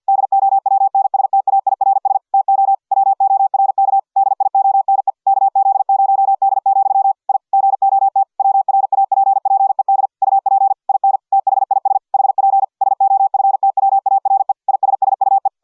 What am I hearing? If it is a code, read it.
Morse code, heard as a keyed tone, about 50 wpm, 780 Hz. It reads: DOWNSTAIRS TO WORK DIONE KZ0D, I GOT WRAPPED UP IN THIS HP SOFTWARE ISSUE